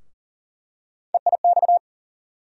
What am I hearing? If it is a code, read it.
Morse code, heard as a keyed tone, about 40 wpm, 700 Hz. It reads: EIX
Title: USA Call Sign Postfixes - 3 Letters 40wpm